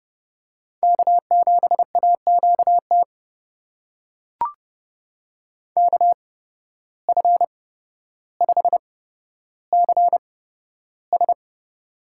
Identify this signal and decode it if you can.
Morse code, heard as a keyed tone, about 30 wpm, spaced as for 13 wpm, 700 Hz. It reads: K7AQT KF5CS
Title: USA Call Signs - 8x Character Spacing - 30wpm